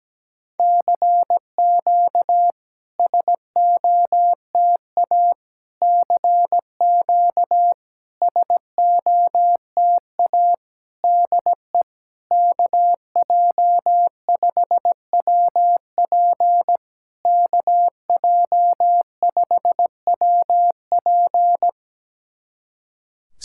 Morse code, heard as a keyed tone, about 17 wpm, 700 Hz. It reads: CQ SOTA CQ SOTA DE KJ5WP KJ5WP
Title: SOTA QSO Practice 17wpm